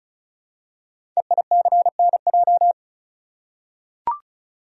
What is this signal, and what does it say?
Morse code, heard as a keyed tone, about 36 wpm, 690 Hz. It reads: EICNJ
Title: Code Groups - 5 Letters - 35wpm